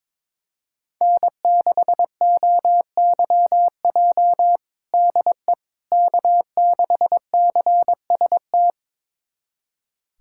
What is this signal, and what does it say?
Morse code, heard as a keyed tone, about 22 wpm, 700 Hz. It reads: N6OYJ DE K6CST
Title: USA Callsign DE Callsign - 5 Characters - 22wpm